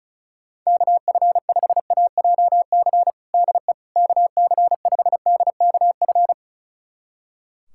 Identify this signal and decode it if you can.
Morse code, heard as a keyed tone, about 36 wpm, 690 Hz. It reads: KF5AJC DE KC5DKF
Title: USA Callsign DE Callsign - 6 Characters - 35wpm